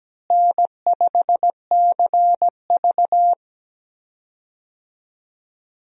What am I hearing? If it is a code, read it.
Morse code, heard as a keyed tone, about 17 wpm, 700 Hz. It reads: N5CV
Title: USA Call Signs - 4 Characters - Repeated 3x - 17wpm